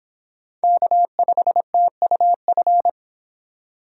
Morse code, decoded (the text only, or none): K5TUF